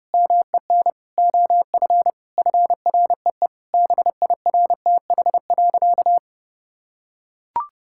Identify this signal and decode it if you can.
Morse code, heard as a keyed tone, about 32 wpm, 700 Hz. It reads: MEN OF FREE BIRTH.
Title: Sentences from Top 1000 Words - Review Part 5 - 30wpm